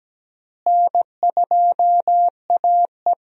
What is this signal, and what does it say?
Morse code, heard as a keyed tone, about 17 wpm, 700 Hz. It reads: N2AE
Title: USA Call Signs - 4 Characters - Repeated 3x - 17wpm